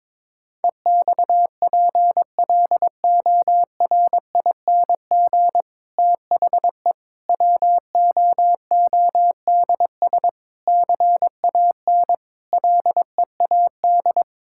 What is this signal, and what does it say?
Morse code, heard as a keyed tone, about 22 wpm, 700 Hz. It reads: EXPLORING THE WOODS CAN LEAD
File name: Difficult Sentences with Hints - 22wpm